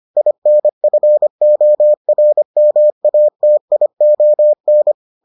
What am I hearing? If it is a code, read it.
Morse code, heard as a keyed tone, about 25 wpm, 600 Hz. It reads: INFORMATION